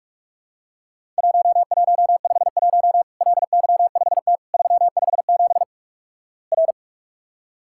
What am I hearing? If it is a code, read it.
Morse code, heard as a keyed tone, about 46 wpm, 700 Hz. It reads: 1151 LY5T 357 R
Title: Worked All Europe DX Contest - QTC Exchange - 45wpm